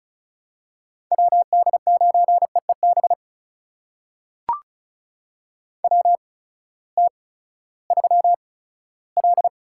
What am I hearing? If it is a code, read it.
Morse code, heard as a keyed tone, about 36 wpm, spaced as for 18 wpm, 700 Hz. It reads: WD9EEB WT3L
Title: USA Call Signs - 8x Character Spacing - 35wpm